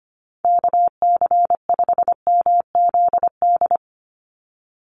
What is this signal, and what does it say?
Morse code, heard as a keyed tone, about 25 wpm, 700 Hz. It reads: KC5MZD
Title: USA Call Signs 25wpm